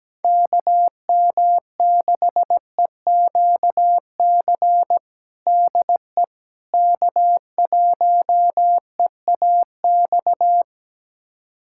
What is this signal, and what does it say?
Morse code, heard as a keyed tone, about 17 wpm, 700 Hz. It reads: KM6EQC DE K1EAX
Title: USA Callsign DE Callsign 17wpm